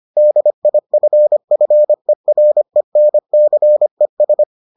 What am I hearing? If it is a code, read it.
Morse code, heard as a keyed tone, about 25 wpm, 600 Hz. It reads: DIFFERENCES